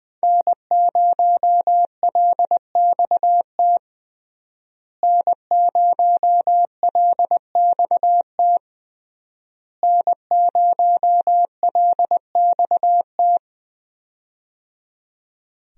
Morse code, heard as a keyed tone, about 21 wpm, 700 Hz. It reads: N0LXT N0LXT N0LXT
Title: USA Call Signs - 5 Characters - Repeated 3x - 20wpm